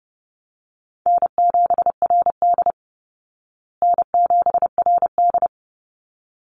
Morse code, heard as a keyed tone, about 30 wpm, 700 Hz. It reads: N7RD N7RD
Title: Easy Call Signs Repeated 2x - 30wpm